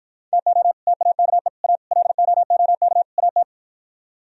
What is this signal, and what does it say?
Morse code, heard as a keyed tone, about 52 wpm, 700 Hz. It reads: TO TAKE A LOOK AT